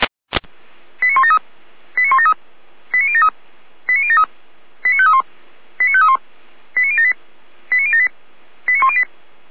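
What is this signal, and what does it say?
Five-tone selective call (ZVEI), heard as quick runs of five tones, tones of about 70 ms. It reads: 89173 89173 89073 89073 89531 89531 89078 89078 89108
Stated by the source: ZVEI five-tone selective call 89337 received by SDR, fire brigade alerting (feuerwehr-orlamuende).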